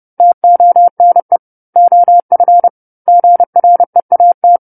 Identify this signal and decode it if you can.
Morse code, heard as a keyed tone, about 30 wpm, 700 Hz. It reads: TONE OF GREAT
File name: 10-Chapter-10 30 WPM